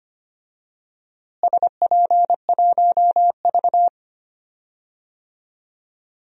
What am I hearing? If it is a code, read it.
Morse code, heard as a keyed tone, about 25 wpm, 700 Hz. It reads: SP1V
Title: DXCC-100 Call Signs - 4 Characters - Repeated 2x - 25wpm